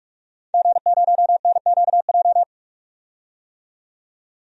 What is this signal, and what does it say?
Morse code, heard as a keyed tone, about 43 wpm, 700 Hz. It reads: G0NQJ
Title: DXCC-100 Call Signs - 5 Characters - 45wpm